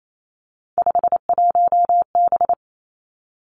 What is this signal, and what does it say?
Morse code, heard as a keyed tone, about 28 wpm, 690 Hz. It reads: 51B
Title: Three Challenging Letter-Number Combinations 28wpm